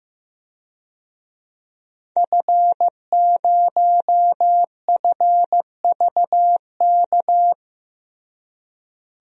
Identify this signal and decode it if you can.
Morse code, heard as a keyed tone, about 15 wpm, 700 Hz. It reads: F0FVK
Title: DXCC-100 Call Signs - 5 Characters - Repeated 3x - 15wpm